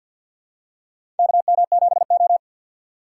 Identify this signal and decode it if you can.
Morse code, heard as a keyed tone, about 52 wpm, 700 Hz. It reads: KM7O
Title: USA Call Signs - 4 Characters - 6x Character Spacing - 50wpm